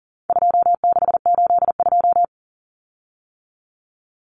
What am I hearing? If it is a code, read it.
Morse code, heard as a keyed tone, about 40 wpm, 700 Hz. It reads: 2682
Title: Common Cut Numbers - 4-Digit 40wpm